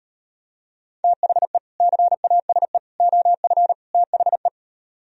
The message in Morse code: THE CASE OF THE

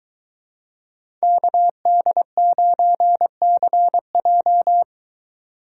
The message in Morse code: KD9CJ

